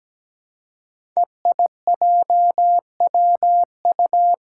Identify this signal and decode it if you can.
Morse code, heard as a keyed tone, about 17 wpm, 700 Hz. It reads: EIJWU
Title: Five Challenging Letter-Number Combinations 17wpm